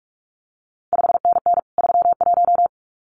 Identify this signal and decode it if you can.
Morse code, heard as a keyed tone, about 46 wpm, 700 Hz. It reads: 5NN 31
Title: CQ WW DX Contest Exchange - 45wpm